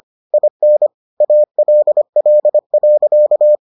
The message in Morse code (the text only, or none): IN ALL.